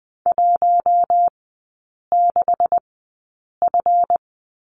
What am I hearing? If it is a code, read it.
Morse code, heard as a keyed tone, about 20 wpm, spaced as for 8 wpm, 700 Hz. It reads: 16F
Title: 138 Lesson 23e - Letter f - Three Characters 20wpm